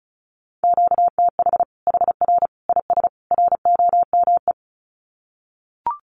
Morse code, heard as a keyed tone, about 34 wpm, 700 Hz. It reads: QTH HR IS ROME
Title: Common QSO Phrases 35wpm